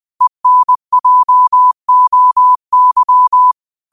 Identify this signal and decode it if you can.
Morse code, heard as a keyed tone, about 20 wpm, 1000 Hz. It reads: ENJOY